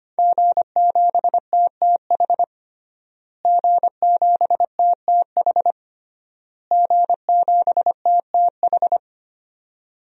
Morse code, heard as a keyed tone, about 25 wpm, 700 Hz. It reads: G7TTH G7TTH G7TTH
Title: DXCC-100 Call Signs - 5 Characters - Repeated 3x - 25wpm